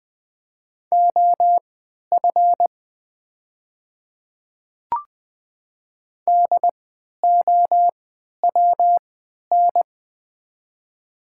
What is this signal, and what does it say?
Morse code, heard as a keyed tone, about 20 wpm, spaced as for 11 wpm, 700 Hz. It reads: OF DOWN
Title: 3x Character Spacing - Top 100 Words - 20wpm